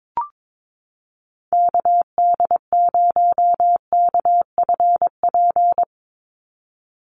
Morse code, heard as a keyed tone, about 22 wpm, 700 Hz. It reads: KD0KFP
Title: USA Call Signs 22wpm